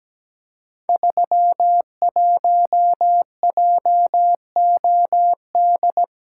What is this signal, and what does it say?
Morse code, heard as a keyed tone, about 17 wpm, 700 Hz. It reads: 31JOD
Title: Code Groups - 5 Letter:Numbers - 17wpm